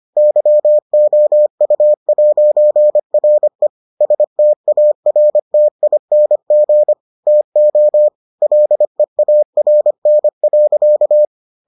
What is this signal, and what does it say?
Morse code, heard as a keyed tone, about 25 wpm, 600 Hz. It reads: YOU'RE STARTING TO LEARN.